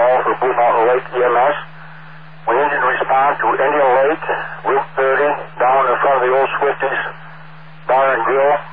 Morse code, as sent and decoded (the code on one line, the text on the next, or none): none